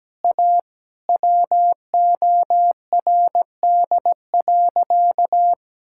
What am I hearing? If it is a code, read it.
Morse code, heard as a keyed tone, about 17 wpm, 700 Hz. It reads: A WORD.
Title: Sentences from Top 1000 Words - Review Part 6 - 17wpm